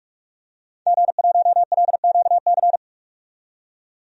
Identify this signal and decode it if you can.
Morse code, heard as a keyed tone, about 46 wpm, 700 Hz. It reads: G1LQC
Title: DXCC-100 Call Signs - 5 Characters - 45wpm